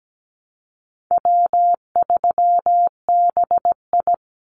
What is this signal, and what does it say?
Morse code, heard as a keyed tone, about 17 wpm, 700 Hz. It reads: W3BI
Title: Easy Call Signs - 17wpm